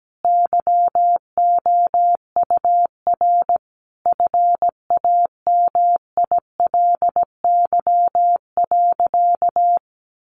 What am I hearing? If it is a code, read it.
Morse code, heard as a keyed tone, about 17 wpm, 700 Hz. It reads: YOUR FAMILY.
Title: Sentences from Top 1400 Words 17wpm